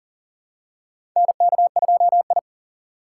Morse code, heard as a keyed tone, about 37 wpm, 700 Hz. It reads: NK2I
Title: USA Call Signs - 4 Characters - 2x Character Spacing - 40wpm